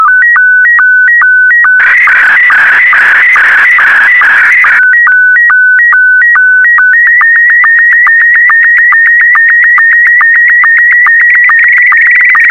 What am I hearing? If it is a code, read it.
SSTV (slow-scan television), Scottie 1.